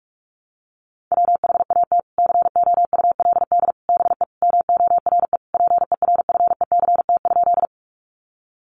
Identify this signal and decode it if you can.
Morse code, heard as a keyed tone, about 44 wpm, 700 Hz. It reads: WHAT COULD BE MORE PERFECT?
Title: Sentences from Top 1000 Words 45wpm